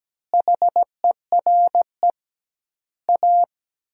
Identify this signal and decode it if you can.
Morse code, heard as a keyed tone, about 17 wpm, 700 Hz. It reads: HERE A
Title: 2x Word-Spacing - Sentences from Top 500 Words - 17wpm